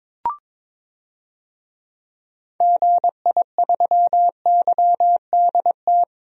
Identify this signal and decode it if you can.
Morse code, heard as a keyed tone, about 22 wpm, 700 Hz. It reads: GI3YDT
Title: DXCC-100 Call Signs - 6 Characters - Repeated 2x - 22wpm